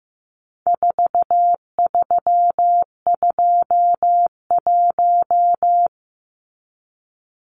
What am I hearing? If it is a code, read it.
Morse code, heard as a keyed tone, about 15 wpm, 700 Hz. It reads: 4321
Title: Numbers - 4-Digit 15wpm